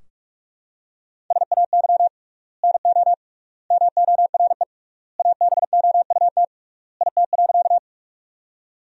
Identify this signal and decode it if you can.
Morse code, heard as a keyed tone, about 46 wpm, 700 Hz. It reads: SAY NO MORE ABOUT IT.
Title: 3x Word-Spacing - Sentences from Top 200 Words - 45wpm